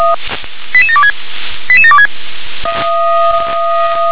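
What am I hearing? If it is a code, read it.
Five-tone selective call (ZVEI), heard as quick runs of five tones, tones of about 70 ms. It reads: 88726 88726
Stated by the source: ZVEI five-tone selective call received on SDR.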